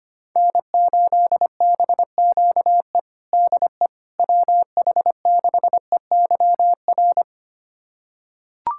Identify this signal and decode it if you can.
Morse code, heard as a keyed tone, about 25 wpm, 700 Hz. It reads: N8BQE DE WH6EYR
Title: USA Callsign DE Callsign 25wpm